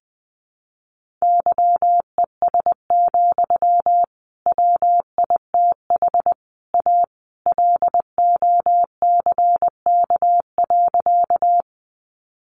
Morse code, heard as a keyed tone, about 20 wpm, 700 Hz. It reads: YES, WITH A LOCK.
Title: Sentences from Top 1300 Words 20wpm